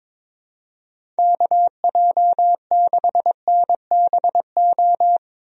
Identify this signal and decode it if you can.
Morse code, heard as a keyed tone, about 22 wpm, 700 Hz. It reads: KJ6NBO
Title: USA Call Signs - 6 Characters - Repeated 2x - 22wpm